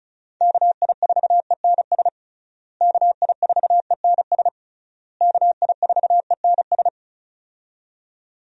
Morse code, heard as a keyed tone, about 34 wpm, 700 Hz. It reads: KI4ENS KI4ENS KI4ENS
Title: USA Call Signs - 6 Characters - Repeated 3x - 35wpm